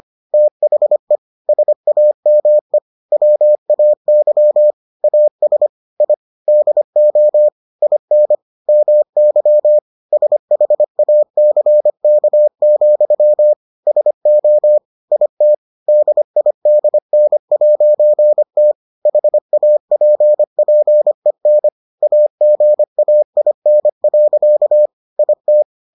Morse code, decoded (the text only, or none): THE SAME WAY AS I DO IN MY SHACK, SO IT DIDN'T HAPPEN AGAIN. IT